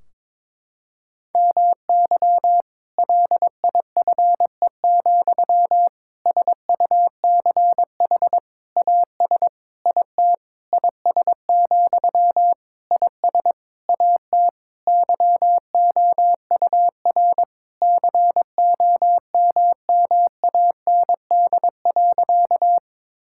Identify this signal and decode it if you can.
Morse code, heard as a keyed tone, about 22 wpm, 700 Hz. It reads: MY LIFE, SUCH AS IT IS, IS AT YOUR COMMAND.
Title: Sentences from Top 1600 Words 22wpm